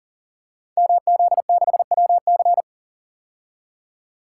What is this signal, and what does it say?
Morse code, heard as a keyed tone, about 40 wpm, 700 Hz. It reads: MZ6WC
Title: Code Groups - 5 Letter:Numbers - 40wpm